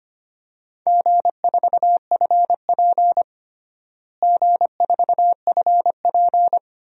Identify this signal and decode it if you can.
Morse code, heard as a keyed tone, about 25 wpm, 700 Hz. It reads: G4FP G4FP
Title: DXCC-100 Call Signs - 4 Characters - Repeated 2x - 25wpm